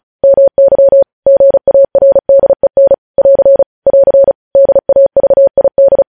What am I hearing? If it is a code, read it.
Morse code, heard as a keyed tone, about 36 wpm, 550 Hz. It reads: MY GARDEN <AR> <AR> DAVID
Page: news headlines in morse code at 35 wpm